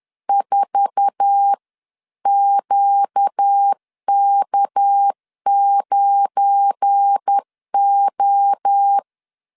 Morse code, heard as a keyed tone, about 10 wpm, 800 Hz. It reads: 4QK9O
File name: CW / DET / 1 9 1500C S A 10wpm 4511